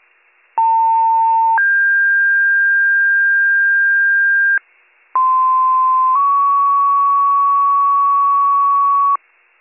Two-tone page, heard as a one-second tone, then a three-second tone, about 900 then 1600 Hz, about 1040 then 1120 Hz. It is heard from 0.6 to 9.2 s.